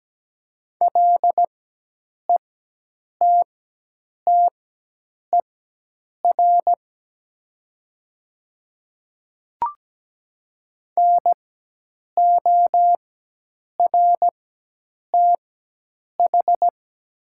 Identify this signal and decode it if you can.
Morse code, heard as a keyed tone, about 17 wpm, spaced as for 8 wpm, 700 Hz. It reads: LETTER NORTH
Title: 4x Character Spacing - Top 300 Words - 17wpm